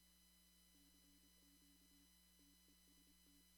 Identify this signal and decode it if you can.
Morse code, heard as a keyed tone, about 25 wpm, 280 Hz, with noise. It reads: CQ TEST
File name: CW / kt5j wpx cw-op-k5pi